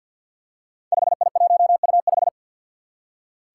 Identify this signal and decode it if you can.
Morse code, heard as a keyed tone, about 48 wpm, 700 Hz. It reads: 5I1U5
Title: Five Challenging Letter-Number Combinations 50wpm